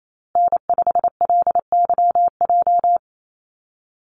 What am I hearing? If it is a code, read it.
Morse code, heard as a keyed tone, about 28 wpm, 700 Hz. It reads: N5LYJ
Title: Call Signs Repeated 3x - 28wpm